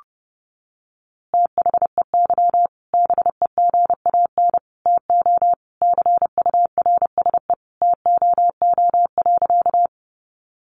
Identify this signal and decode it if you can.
Morse code, heard as a keyed tone, about 30 wpm, 700 Hz. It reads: THEY BEGAN TO CURSE TOO.
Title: A Sound Of Thunder 30wpm